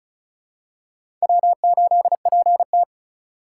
Morse code, heard as a keyed tone, about 36 wpm, 700 Hz. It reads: W8PT
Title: USA Call Signs - 4x Character Spacing - 35wpm